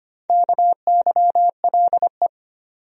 Morse code, heard as a keyed tone, about 25 wpm, 700 Hz. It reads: KYLE
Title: USA Common Names - Max 4 Letters - 25wpm